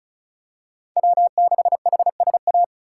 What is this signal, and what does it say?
Morse code, heard as a keyed tone, about 36 wpm, 700 Hz. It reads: W6HSA
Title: USA Call Signs - 8x Character Spacing - 35wpm